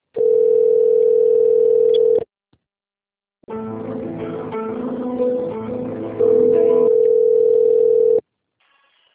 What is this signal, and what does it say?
Ringback tone (North America), 440 + 480 Hz.